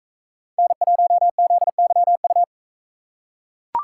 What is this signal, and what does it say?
Morse code, heard as a keyed tone, about 40 wpm, 700 Hz. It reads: N1ZYU